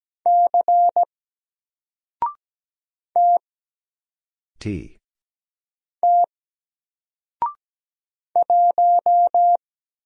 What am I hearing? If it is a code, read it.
Morse code, heard as a keyed tone, about 17 wpm, 700 Hz. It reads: C T T 1
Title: Single Character (Letter:Number:Punctuation:Prosign) Alt 17wpm